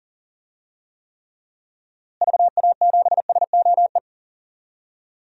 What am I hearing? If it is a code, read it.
Morse code, heard as a keyed tone, about 40 wpm, 700 Hz. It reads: VA7SOE
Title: DXCC-100 Call Signs - 6 Characters - Repeated 2x - 40wpm